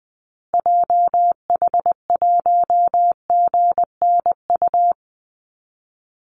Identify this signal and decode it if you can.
Morse code, heard as a keyed tone, about 20 wpm, 700 Hz. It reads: JH1GNU
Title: Call Signs Repeated 2x - 20wpm